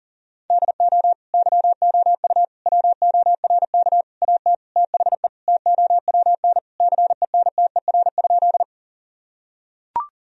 Morse code, heard as a keyed tone, about 40 wpm, 700 Hz. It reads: DO YOU WORK AT THE TOWN CENTER?
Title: Sentences from Top 1000 Words - Review Part 2 - 40wpm